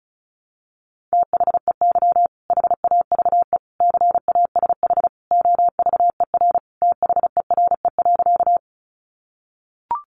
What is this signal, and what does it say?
Morse code, heard as a keyed tone, about 34 wpm, 700 Hz. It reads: THEY HAVE CASH OVER THERE.
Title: Sentences from Top 700 Words 35wpm